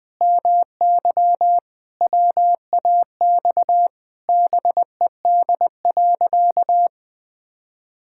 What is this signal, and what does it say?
Morse code, heard as a keyed tone, about 20 wpm, 700 Hz. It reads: MY WAX BED.
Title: Pangram Challenge v2 - 20wpm